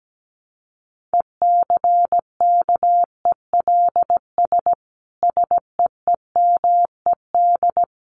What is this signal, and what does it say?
Morse code, heard as a keyed tone, about 17 wpm, 700 Hz. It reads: ECKELS SEEMED
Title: A Sound Of Thunder 17wpm